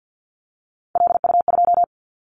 Morse code, heard as a keyed tone, about 50 wpm, 700 Hz. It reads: LU2
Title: Callsign Prefixes for Top 100 DXCC - 50wpm